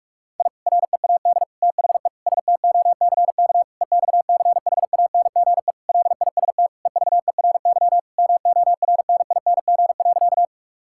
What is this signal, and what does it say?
Morse code, heard as a keyed tone, about 41 wpm, 700 Hz. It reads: I READ THE STOCK EXCHANGE LIST EVERY MORNING.
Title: Sentences from Top 1000 Words - Review Part 6 - 45wpm